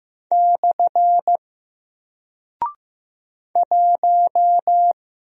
Morse code, heard as a keyed tone, about 15 wpm, 700 Hz. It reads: / 1
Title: Single Character (Letter:Number:Punctuation:Prosign) Alt 15wpm